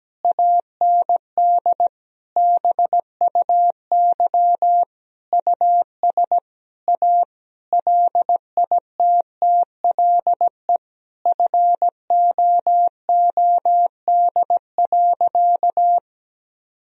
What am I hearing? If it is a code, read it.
Morse code, heard as a keyed tone, about 17 wpm, 700 Hz. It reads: AND BUY US A LITTLE FOOD.
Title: Sentences from Top 1000 Words - Review Part 4 - 17wpm